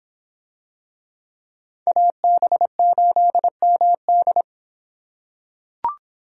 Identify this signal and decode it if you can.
Morse code, heard as a keyed tone, about 25 wpm, 700 Hz. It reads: AB8MD